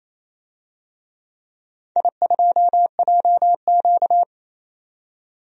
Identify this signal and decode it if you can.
Morse code, heard as a keyed tone, about 28 wpm, 700 Hz. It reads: I2JQ
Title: DXCC-100 Call Signs - 4 Characters - 28wpm